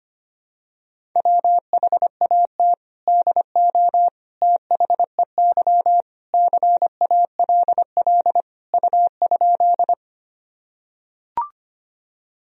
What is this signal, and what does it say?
Morse code, heard as a keyed tone, about 25 wpm, 700 Hz. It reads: WHAT DO THEY CALL U?
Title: Sentences from Top 100 Words and Abbreviations 25wpm